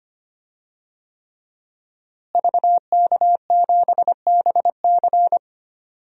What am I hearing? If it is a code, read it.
Morse code, heard as a keyed tone, about 25 wpm, 700 Hz. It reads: VK7BC
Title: DXCC-100 Call Signs - 5 Characters - Repeated 3x - 25wpm